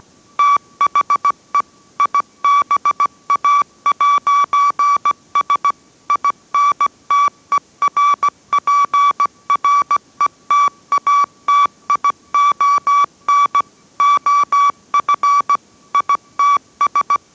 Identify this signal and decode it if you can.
Morse code, heard as a keyed tone, about 17 wpm, 1150 Hz. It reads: THE IBA'S INTERPRETATION OF ITS